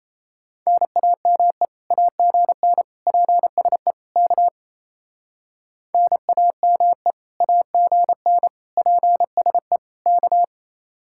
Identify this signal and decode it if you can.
Morse code, heard as a keyed tone, about 34 wpm, 700 Hz. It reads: NAME AGN PSE K NAME AGN PSE K
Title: Speed-Racing Common QSO Phrases Encore - 28wpm